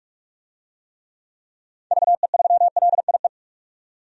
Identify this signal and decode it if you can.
Morse code, heard as a keyed tone, about 46 wpm, 700 Hz. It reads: VE3LIE
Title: DXCC-100 Call Signs - 6 Characters - 45wpm